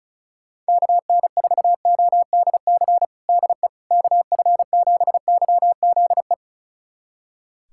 Morse code, heard as a keyed tone, about 34 wpm, 700 Hz. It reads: KN4ODC DE KF7YZE
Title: USA Callsign DE Callsign 35wpm